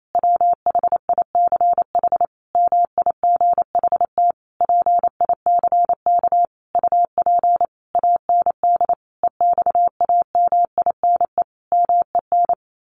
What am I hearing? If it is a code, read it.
Morse code, heard as a keyed tone, about 28 wpm, 700 Hz. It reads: WHICH MIGHT PICK UP AND EXAMINE MEN
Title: A Sound Of Thunder 28wpm